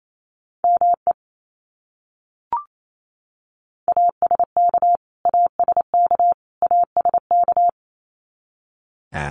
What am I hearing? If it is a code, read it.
Morse code, heard as a keyed tone, about 28 wpm, 700 Hz. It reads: ME ASK ASK ASK
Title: Top 200 Words Repeated 28wpm